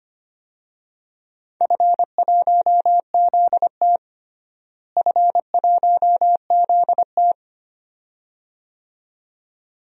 Morse code, heard as a keyed tone, about 25 wpm, 700 Hz. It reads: F1ZT F1ZT
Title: DXCC-100 Call Signs - 4 Characters - Repeated 2x - 25wpm